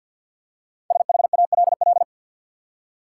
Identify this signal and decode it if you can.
Morse code, heard as a keyed tone, about 50 wpm, 700 Hz. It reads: SHALL
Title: Five Letter Words - 50wpm